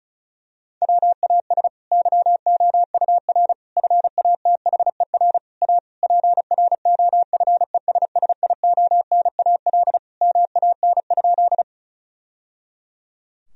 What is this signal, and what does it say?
Morse code, heard as a keyed tone, about 34 wpm, 700 Hz. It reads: WAS YOUR FATHER A PROFESSIONAL MAN?
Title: Sentences from Top 1000 Words - Review Part 4 - 35wpm